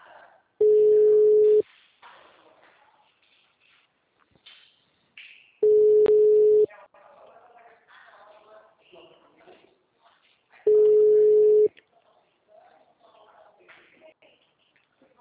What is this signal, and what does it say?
Ringback tone (Europe, 425 Hz), with speech.